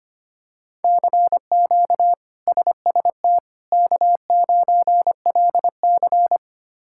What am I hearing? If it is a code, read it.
Morse code, heard as a keyed tone, about 25 wpm, 700 Hz. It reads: CQ SST K9LC
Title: K1USN SST QSO Practice - 25wpm